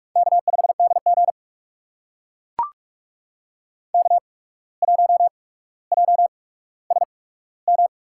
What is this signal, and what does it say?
Morse code, heard as a keyed tone, about 44 wpm, spaced as for 18 wpm, 700 Hz. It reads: K5DG K1JSM
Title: USA Call Signs - 8x Character Spacing - 45wpm